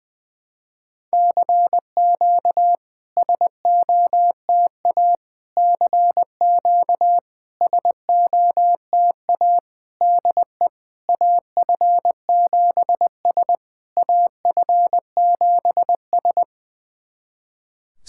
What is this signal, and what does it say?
Morse code, heard as a keyed tone, about 20 wpm, 700 Hz. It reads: CQ SOTA CQ SOTA DE AF7S AF7S
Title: SOTA QSO Practice 20wpm